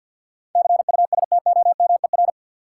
Morse code, heard as a keyed tone, about 50 wpm, 700 Hz. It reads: CUSTOMER
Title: Sentences from Top 1100 Words - New Word Drill Alt - 50wpm